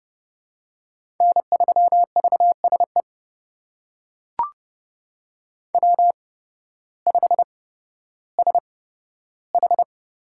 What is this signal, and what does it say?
Morse code, heard as a keyed tone, about 30 wpm, spaced as for 14 wpm, 700 Hz. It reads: N3VSE W5SH
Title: USA Call Signs - 5 Characters - 8x Character Spacing - 30wpm